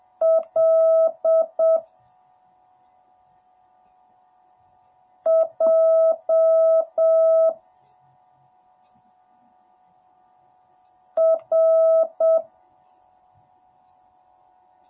Morse code, heard as a keyed tone, about 7 wpm, 650 Hz. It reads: L J R